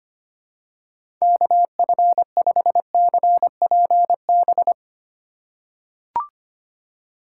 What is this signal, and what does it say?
Morse code, heard as a keyed tone, about 25 wpm, 700 Hz. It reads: KF5CPB
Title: USA Call Signs - 6 Characters - 3x Character Spacing - 25wpm